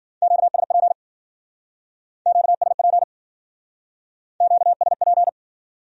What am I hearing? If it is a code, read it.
Morse code, heard as a keyed tone, about 52 wpm, 700 Hz. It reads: QSP QSP QSP